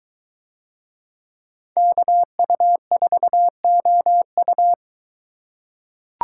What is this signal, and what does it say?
Morse code, heard as a keyed tone, about 22 wpm, 700 Hz. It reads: KU4OU